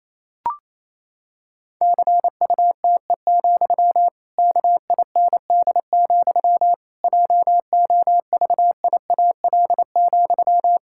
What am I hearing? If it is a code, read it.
Morse code, heard as a keyed tone, about 28 wpm, 700 Hz. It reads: CUTE, KIND, JOVIAL,
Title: Pangram Challenge v2 - 28wpm